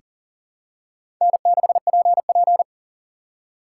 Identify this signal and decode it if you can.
Morse code, heard as a keyed tone, about 40 wpm, 700 Hz. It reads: N6PP
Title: USA Call Signs - 4 Characters - 6x Character Spacing - 40wpm